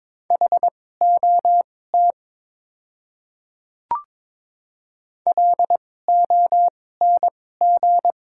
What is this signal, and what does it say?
Morse code, heard as a keyed tone, about 21 wpm, spaced as for 16 wpm, 700 Hz. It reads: HOT LONG
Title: 2x Character Spacing - Top 200 Words - 22wpm